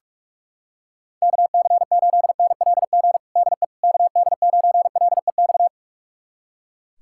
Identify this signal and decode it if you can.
Morse code, heard as a keyed tone, about 41 wpm, 700 Hz. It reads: KC8NLG DE KD9LEX
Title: USA Callsign DE Callsign - 6 Characters - 45wpm